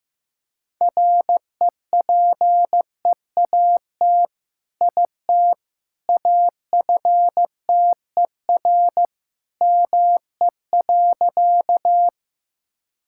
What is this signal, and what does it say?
Morse code, heard as a keyed tone, about 15 wpm, 700 Hz. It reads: REPEAT IT AFTER ME.
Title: Sentences from Top 1600 Words 15wpm